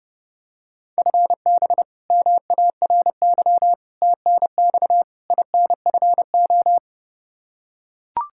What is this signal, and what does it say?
Morse code, heard as a keyed tone, about 30 wpm, 700 Hz. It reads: FB MARY TNX INFO
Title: Practice QSO - Exchange 3 - More Info - 30wpm